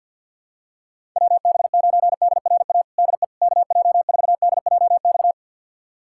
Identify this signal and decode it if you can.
Morse code, heard as a keyed tone, about 50 wpm, 700 Hz. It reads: WB9DRA DE KJ4DJX
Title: USA Callsign DE Callsign - 6 Characters - 50wpm